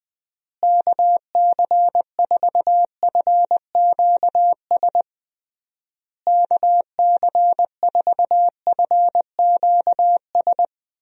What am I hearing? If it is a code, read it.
Morse code, heard as a keyed tone, about 21 wpm, 700 Hz. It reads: KC4FQS KC4FQS
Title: USA Call Signs - 6 Characters - Repeated 3x - 20wpm